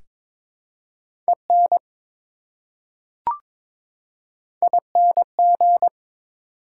Morse code, heard as a keyed tone, about 21 wpm, 700 Hz. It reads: EN ING
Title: Prefixes and Suffixes 22wpm